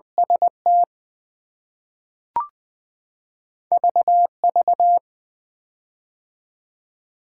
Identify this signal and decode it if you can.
Morse code, heard as a keyed tone, about 20 wpm, 700 Hz. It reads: ST VV
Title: K1USN SST QSO Practice - 20wpm